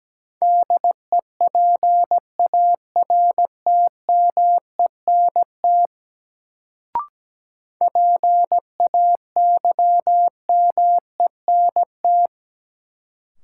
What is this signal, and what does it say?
Morse code, heard as a keyed tone, about 17 wpm, 700 Hz. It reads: DEPARTMENT PAYMENT
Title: Sentences from Top 1200 Words - New Word Drill Alt - 17wpm